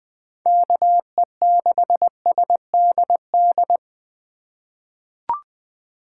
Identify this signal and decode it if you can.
Morse code, heard as a keyed tone, about 20 wpm, 700 Hz. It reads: KE6SDD
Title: USA Call Signs - 3x Character Spacing - 20wpm